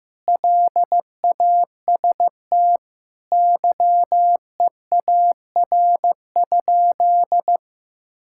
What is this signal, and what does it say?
Morse code, heard as a keyed tone, about 15 wpm, 700 Hz. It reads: LAST YEAR?
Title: Sentences from Top 1000 Words - Review Part 6 - 15wpm